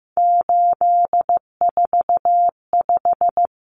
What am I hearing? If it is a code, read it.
Morse code, heard as a keyed tone, about 15 wpm, 700 Hz. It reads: 845
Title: Common Cut Numbers - 3-Digit 15wpm